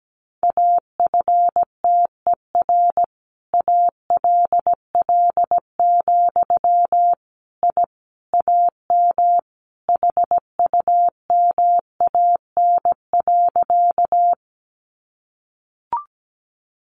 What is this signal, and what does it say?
Morse code, heard as a keyed tone, about 17 wpm, 700 Hz. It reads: AFTER ALL, I AM HUMAN.
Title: Sentences from Top 600 Words 17wpm